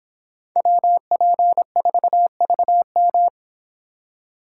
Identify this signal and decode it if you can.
Morse code, heard as a keyed tone, about 24 wpm, 700 Hz. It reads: WP4VM